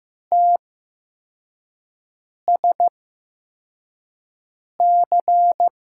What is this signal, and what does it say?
Morse code, heard as a keyed tone, about 15 wpm, spaced as for 4 wpm, 700 Hz. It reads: TSC
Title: USA Call Signs - 6 Characters - 8x Character Spacing - 15wpm